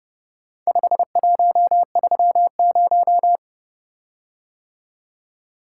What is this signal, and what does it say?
Morse code, heard as a keyed tone, about 30 wpm, 700 Hz. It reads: 5130